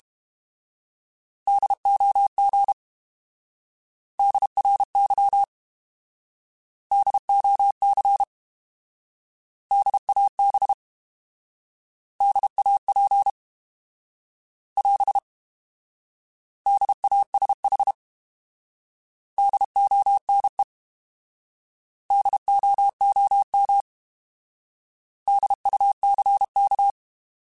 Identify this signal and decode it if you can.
Morse code, heard as a keyed tone, about 32 wpm, 800 Hz. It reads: DOG DRY DOC DAB DAP <AS> DASH DONE DOOM DUCK